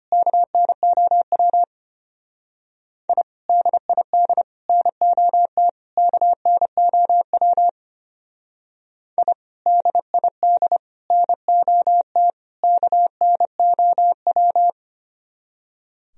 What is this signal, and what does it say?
Morse code, heard as a keyed tone, about 31 wpm, 690 Hz. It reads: KNOW I DID NOT KNOW I DID NOT KNOW
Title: Speed-Racing - Sets of 4 Words 25wpm